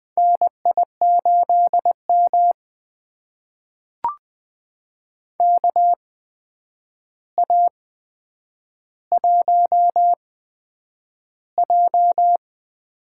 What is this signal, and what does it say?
Morse code, heard as a keyed tone, about 20 wpm, spaced as for 9 wpm, 700 Hz. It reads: NI8M KA1J
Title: USA Call Signs - 8x Character Spacing - 20wpm